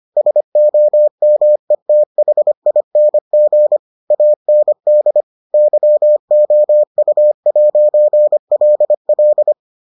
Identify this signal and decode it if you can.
Morse code, heard as a keyed tone, about 25 wpm, 600 Hz. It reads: SOMETHING AND YOU'LL